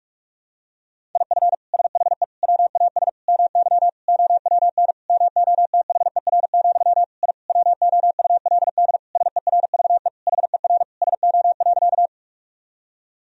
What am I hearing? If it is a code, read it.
Morse code, heard as a keyed tone, about 47 wpm, 700 Hz. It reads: IF SHE WAS MY OWN MOTHER, I WOULD SERVE HER SO.
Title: Sentences from Top 1000 Words - Review Part 2 - 45wpm